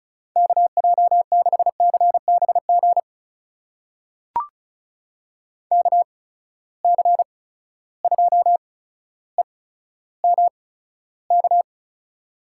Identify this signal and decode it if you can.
Morse code, heard as a keyed tone, about 36 wpm, spaced as for 15 wpm, 700 Hz. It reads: KJ6CBG KC2EMK
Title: USA Call Signs - 6 Characters - 8x Character Spacing - 35wpm